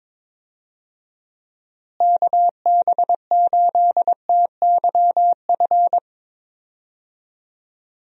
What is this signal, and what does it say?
Morse code, heard as a keyed tone, about 22 wpm, 700 Hz. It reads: KB8TYF
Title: USA Call Signs - 6 Characters - Repeated 2x - 22wpm